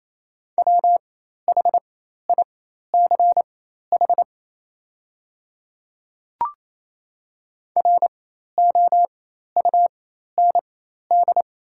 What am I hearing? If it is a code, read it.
Morse code, heard as a keyed tone, about 28 wpm, spaced as for 13 wpm, 700 Hz. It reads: WHICH ROUND
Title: 4x Character Spacing - Top 300 Words - 28wpm